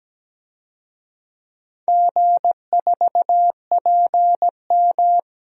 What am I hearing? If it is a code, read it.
Morse code, heard as a keyed tone, about 17 wpm, 700 Hz. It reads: G4PM
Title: DXCC-100 Call Signs - 4 Characters - Repeated 3x - 17wpm